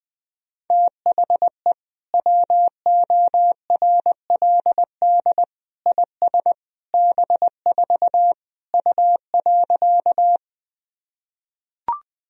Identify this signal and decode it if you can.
Morse code, heard as a keyed tone, about 21 wpm, 700 Hz. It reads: THE WORLD IS B4 U.
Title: Sentences from Top 200 Words and Abbreviations 20wpm